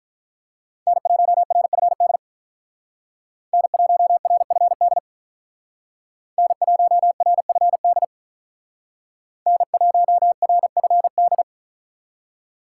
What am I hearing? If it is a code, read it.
Morse code, heard as a keyed tone, about 45 wpm, 700 Hz. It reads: N1RFD N1RFD N1RFD N1RFD